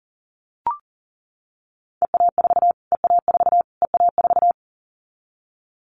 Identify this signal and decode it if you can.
Morse code, heard as a keyed tone, about 40 wpm, 700 Hz. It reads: EA4 EA4 EA4
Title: Callsign Prefixes for Top 25 DXCC - Repeated 3x - 40wpm